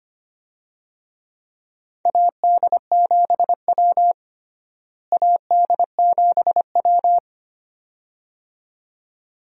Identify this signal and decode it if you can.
Morse code, heard as a keyed tone, about 25 wpm, 700 Hz. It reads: AD7W AD7W
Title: USA Call Signs - 4 Characters - Repeated 2x - 25wpm